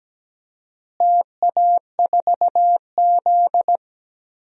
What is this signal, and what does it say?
Morse code, heard as a keyed tone, about 17 wpm, 700 Hz. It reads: TA4Z
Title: DXCC-100 Call Signs - 4 Characters - Repeated 2x - 17wpm